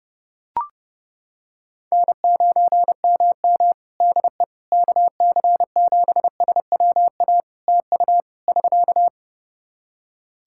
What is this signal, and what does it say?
Morse code, heard as a keyed tone, about 30 wpm, 700 Hz. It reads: N9MM DE KC7SWA TU <SK>
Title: Practice QSO - Exchange 4 - Ending - 30wpm